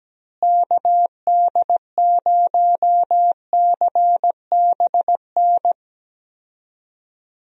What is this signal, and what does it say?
Morse code, heard as a keyed tone, about 17 wpm, 700 Hz. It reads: KD0CBN
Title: USA Call Signs - 6 Characters - Repeated 3x - 17wpm